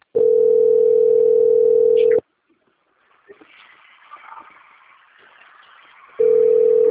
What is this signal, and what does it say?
Ringback tone (North America), 440 + 480 Hz.
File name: out- - - - - .27701